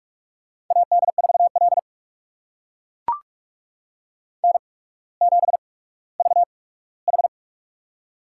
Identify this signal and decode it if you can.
Morse code, heard as a keyed tone, about 44 wpm, spaced as for 19 wpm, 700 Hz. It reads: AD4L N7VH
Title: USA Call Signs - 4 Characters - 8x Character Spacing - 45wpm